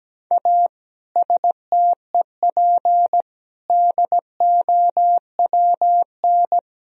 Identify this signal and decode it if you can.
Morse code, heard as a keyed tone, about 17 wpm, 700 Hz. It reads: A STEP DOWN
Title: Sentences from Top 1000 Words - Review Part 4 - 17wpm